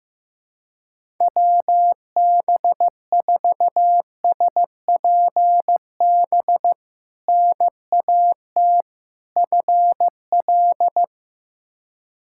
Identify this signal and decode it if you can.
Morse code, heard as a keyed tone, about 15 wpm, 700 Hz. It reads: WB4SPB NAT FL
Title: K1USN SST QSO Practice - 15wpm